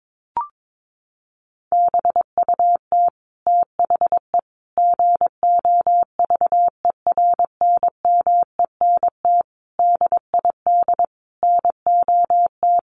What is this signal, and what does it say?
Morse code, heard as a keyed tone, about 22 wpm, 700 Hz. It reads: BUT THE GOVERNMENT DID NOT
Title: Sentences from Top 1400 Words 22wpm